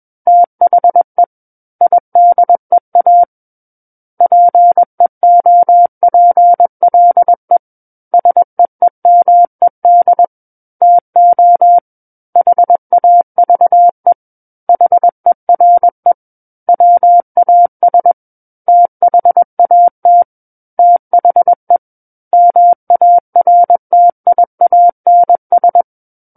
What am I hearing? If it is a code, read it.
Morse code, heard as a keyed tone, about 21 wpm, 700 Hz. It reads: THE IDEA PEOPLE SEEMED TO HAVE HERE WAS THAT THE MARTIANS
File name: Book One / 12-Chapter 12 - 21 WPM